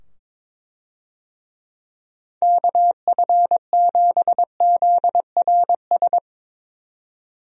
Morse code, heard as a keyed tone, about 22 wpm, 700 Hz. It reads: KF7ZRS
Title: USA Call Signs - 6 Characters - 22wpm